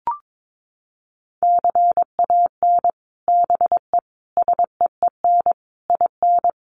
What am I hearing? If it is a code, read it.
Morse code, heard as a keyed tone, about 22 wpm, 700 Hz. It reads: CAN BE SEEN IN
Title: Sets of 5 Words Encore2 22wpm